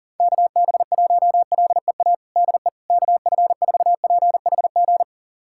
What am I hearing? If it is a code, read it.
Morse code, heard as a keyed tone, about 40 wpm, 700 Hz. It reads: KB1LEA DE KF4PHG
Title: USA Callsign DE Callsign 40wpm